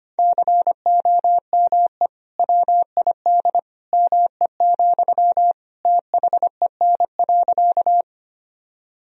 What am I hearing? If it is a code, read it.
Morse code, heard as a keyed tone, about 25 wpm, 700 Hz. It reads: COME WID ME, THEN.
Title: Sentences from Top 200 Words and Abbreviations 25wpm